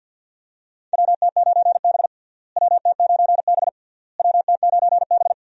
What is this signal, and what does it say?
Morse code, heard as a keyed tone, about 49 wpm, 700 Hz. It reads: WT9B WT9B WT9B
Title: USA Call Signs - 4 Characters - Repeated 3x - 50wpm